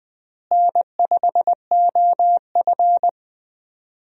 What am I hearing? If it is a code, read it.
Morse code, heard as a keyed tone, about 20 wpm, 700 Hz. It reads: N5OF
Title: USA Call Signs - 4 Characters - Repeated 3x - 20wpm